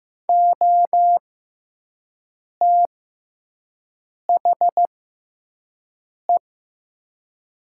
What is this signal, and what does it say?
Morse code, heard as a keyed tone, about 15 wpm, spaced as for 5 wpm, 700 Hz. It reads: OTHE
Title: 6x Character Spacing - Top 100 Words - 15wpm